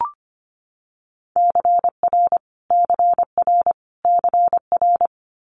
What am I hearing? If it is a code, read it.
Morse code, heard as a keyed tone, about 25 wpm, 700 Hz. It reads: CR CR CR
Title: Multi-letter Phonemes Repeated 3x 25wpm